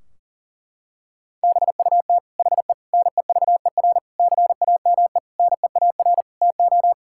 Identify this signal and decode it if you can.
Morse code, heard as a keyed tone, about 40 wpm, 700 Hz. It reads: BUT HE NEVER CAME NEAR TO
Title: Sentences from Top 1600 Words 40wpm